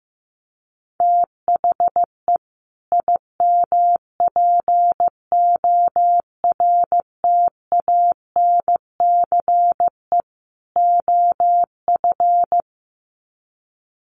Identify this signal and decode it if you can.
Morse code, heard as a keyed tone, about 15 wpm, 700 Hz. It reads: THE IMPORTANCE OF
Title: Sets of 3 Words 15wpm